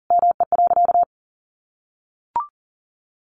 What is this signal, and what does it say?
Morse code, heard as a keyed tone, about 40 wpm, 700 Hz. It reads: ME.
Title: Sentences from Top 1100 Words 40wpm